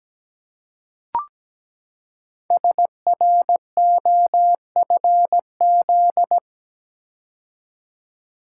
Morse code, heard as a keyed tone, about 17 wpm, 700 Hz. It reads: SROFZ
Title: Code Groups - 5 Letters - 17wpm